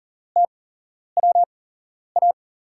Morse code, heard as a keyed tone, about 38 wpm, spaced as for 11 wpm, 700 Hz. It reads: TWA
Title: USA Call Signs - 8x Character Spacing - 40wpm